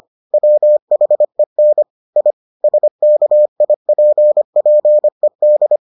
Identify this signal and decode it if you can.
Morse code, heard as a keyed tone, about 25 wpm, 600 Hz. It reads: WHEN I SKIPPED